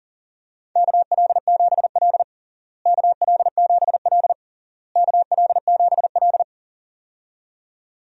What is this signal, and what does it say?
Morse code, heard as a keyed tone, about 37 wpm, 700 Hz. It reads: KL7L KL7L KL7L
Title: DXCC-100 Call Signs - 4 Characters - Repeated 3x - 40wpm